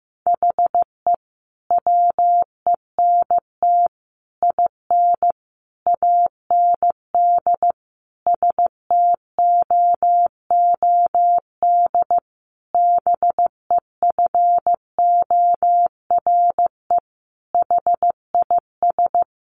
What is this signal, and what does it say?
Morse code, heard as a keyed tone, about 15 wpm, 700 Hz. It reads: HE WENT IN AND STOOD BEFORE HIS
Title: Sentences from Top 1300 Words 15wpm